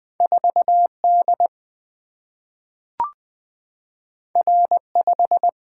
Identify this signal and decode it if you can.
Morse code, heard as a keyed tone, about 20 wpm, 700 Hz. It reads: 4D R5
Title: Single-Letter Single-Number 20wpm